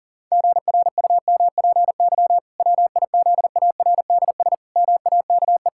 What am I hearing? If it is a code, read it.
Morse code, heard as a keyed tone, about 40 wpm, 700 Hz. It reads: GRUMPY WIZARDS MAKE
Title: Pangram Challenge v2 - 40wpm